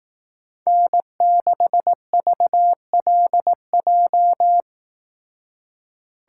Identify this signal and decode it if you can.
Morse code, heard as a keyed tone, about 18 wpm, 700 Hz. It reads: N6VLJ
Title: Speed Racing - USA 1x3 Call Signs 15wpm